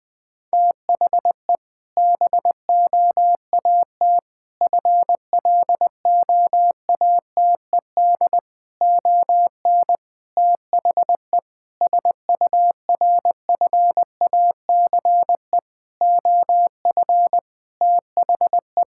Morse code, heard as a keyed tone, about 20 wpm, 700 Hz. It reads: THE BOAT FLOATED ON THE SURFACE OF THE
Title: Sentences with 4-Word Phrases - 20wpm